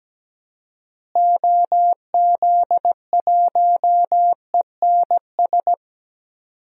Morse code, heard as a keyed tone, about 17 wpm, 700 Hz. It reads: OZ1ENS
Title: DXCC-100 Call Signs - 6 Characters - Repeated 2x - 17wpm